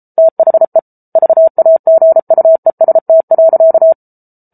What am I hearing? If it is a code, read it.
Morse code, heard as a keyed tone, about 34 wpm, 650 Hz. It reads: THE VAGUEST.